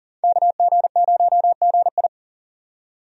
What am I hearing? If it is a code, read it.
Morse code, heard as a keyed tone, about 38 wpm, 700 Hz. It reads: KG0GI
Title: USA Call Signs - 5 Characters - 40wpm